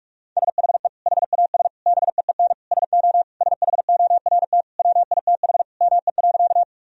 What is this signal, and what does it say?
Morse code, heard as a keyed tone, about 46 wpm, 700 Hz. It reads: SHE HAS BEEN SO SHORT WITH ME.
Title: Sentences from Top 1000 Words - Review Part 2 - 45wpm